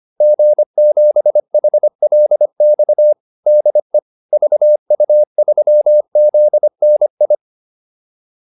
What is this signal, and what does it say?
Morse code, heard as a keyed tone, about 25 wpm, 600 Hz. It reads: G7HLX DE VU3ZNI